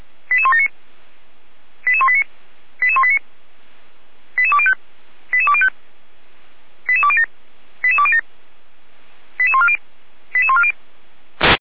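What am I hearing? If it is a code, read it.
Five-tone selective call (ZVEI), heard as quick runs of five tones, tones of about 70 ms. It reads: 88179 88189 88189 88205 88205 88207 88207 88140 88140
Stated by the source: ZVEI five-tone call, code 88169, received by SDR.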